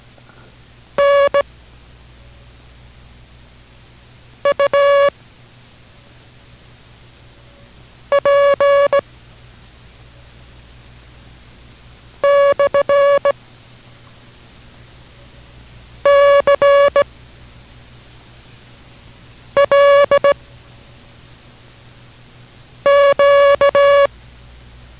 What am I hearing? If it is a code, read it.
Morse code, heard as a keyed tone, about 17 wpm, 560 Hz. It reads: N U P / C L Q